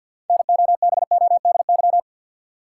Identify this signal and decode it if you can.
Morse code, heard as a keyed tone, about 53 wpm, 700 Hz. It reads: NOBODY
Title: Sentences from Top 1600 Words - New Word Drill Alt - 50wpm